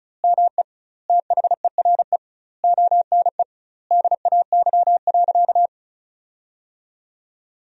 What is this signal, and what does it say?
Morse code, heard as a keyed tone, about 36 wpm, 700 Hz. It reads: ME THERE ONE DAY.
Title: 2x Word-Spacing - Sentences from Top 300 Words - 35wpm